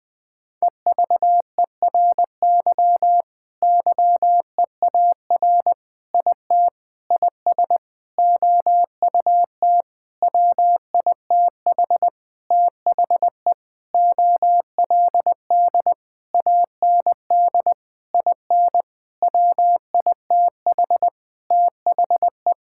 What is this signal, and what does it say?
Morse code, heard as a keyed tone, about 21 wpm, 700 Hz. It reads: EVERY YEAR IT IS OUT WITH THE OLD AND IN WITH THE
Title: Sentences with 9-Word Phrases - 20wpm